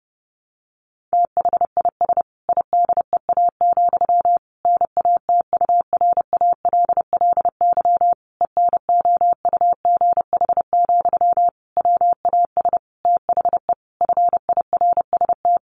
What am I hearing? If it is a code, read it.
Morse code, heard as a keyed tone, about 30 wpm, 700 Hz. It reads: THIS IDEA, NATURALLY ENOUGH, WAS THE FIRST
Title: Sentences from Top 1300 Words 30wpm